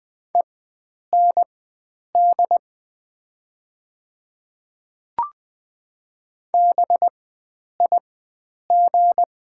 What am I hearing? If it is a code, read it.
Morse code, heard as a keyed tone, about 20 wpm, spaced as for 9 wpm, 700 Hz. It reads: END BIG
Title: 4x Character Spacing - Top 300 Words - 20wpm